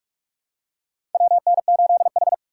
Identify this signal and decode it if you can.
Morse code, heard as a keyed tone, about 42 wpm, 700 Hz. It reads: WN8H
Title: USA Call Signs - 4 Characters - 6x Character Spacing - 45wpm